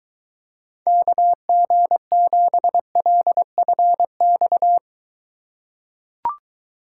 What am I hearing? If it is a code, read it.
Morse code, heard as a keyed tone, about 23 wpm, 700 Hz. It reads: KG7LFX